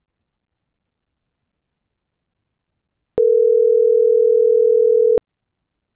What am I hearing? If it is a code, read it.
Ringback tone (North America), 440 + 480 Hz.